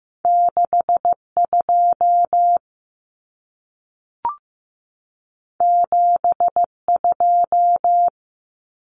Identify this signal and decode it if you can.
Morse code, heard as a keyed tone, about 15 wpm, 700 Hz. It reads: 62 72
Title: Numbers - 2-Digit 15wpm